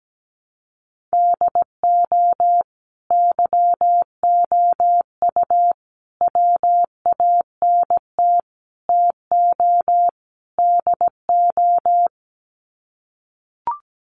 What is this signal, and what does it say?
Morse code, heard as a keyed tone, about 17 wpm, 700 Hz. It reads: DO YOU WANT TO DO
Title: Sets of 5 Words Encore 17wpm